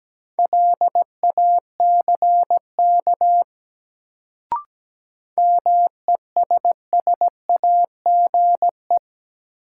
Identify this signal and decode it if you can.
Morse code, heard as a keyed tone, about 17 wpm, 700 Hz. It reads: LACK MESSAGE
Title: Sentences from Top 1100 Words - New Word Drill Alt - 17wpm